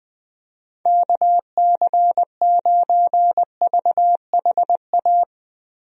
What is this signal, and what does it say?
Morse code, heard as a keyed tone, about 20 wpm, 700 Hz. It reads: KC9VHA
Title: USA Call Signs - 6 Characters - Repeated 3x - 20wpm